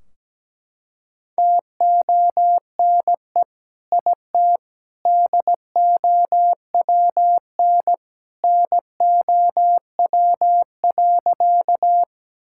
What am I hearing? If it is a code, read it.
Morse code, heard as a keyed tone, about 17 wpm, 700 Hz. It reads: TONE IT DOWN NOW.
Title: Sentences from Top 1600 Words 17wpm